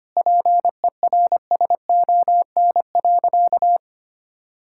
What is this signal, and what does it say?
Morse code, heard as a keyed tone, about 25 wpm, 700 Hz. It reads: PERSON.
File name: practice / 25wpm / Sentences with 3-Word Phrases - 25wpm